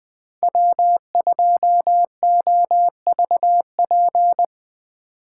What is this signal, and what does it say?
Morse code, heard as a keyed tone, about 23 wpm, 700 Hz. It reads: W2OVP
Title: Speed Racing - USA 1x3 Call Signs 20wpm